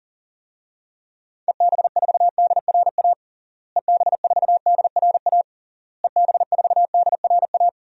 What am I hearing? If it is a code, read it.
Morse code, heard as a keyed tone, about 37 wpm, 700 Hz. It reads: EB4DRA EB4DRA EB4DRA
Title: DXCC-100 Call Signs - 6 Characters - Repeated 3x - 40wpm